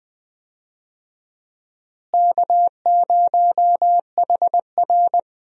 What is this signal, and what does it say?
Morse code, heard as a keyed tone, about 20 wpm, 700 Hz. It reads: K0HR
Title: USA Call Signs - 4 Characters - Repeated 3x - 20wpm